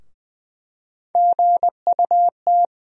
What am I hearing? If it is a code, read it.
Morse code, heard as a keyed tone, about 20 wpm, 700 Hz. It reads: GUT
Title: USA Call Sign Postfixes - 3 Letters 20wpm